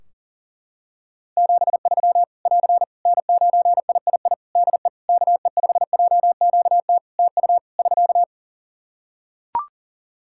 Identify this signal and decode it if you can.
Morse code, heard as a keyed tone, about 40 wpm, 700 Hz. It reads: 73 <AR> N9III DE KE5JQT TU <SK>
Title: Practice QSO - Exchange 4 - Ending - 40wpm